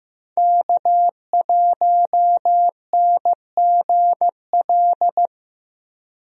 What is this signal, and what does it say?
Morse code, heard as a keyed tone, about 15 wpm, 700 Hz. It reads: K1NGL
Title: USA Call Signs - 5 Characters - Repeated 3x - 15wpm